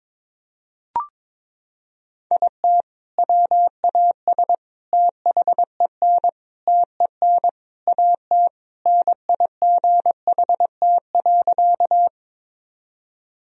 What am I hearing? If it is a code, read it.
Morse code, heard as a keyed tone, about 22 wpm, 700 Hz. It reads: IT WAS THEN TEN AT NIGHT.
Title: Sentences from Top 1000 Words - Review Part 2 - 22wpm